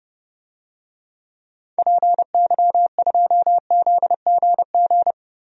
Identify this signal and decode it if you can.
Morse code, heard as a keyed tone, about 32 wpm, 700 Hz. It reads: PY2ZGG
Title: DXCC-100 Call Signs - 6 Characters - Repeated 2x - 30wpm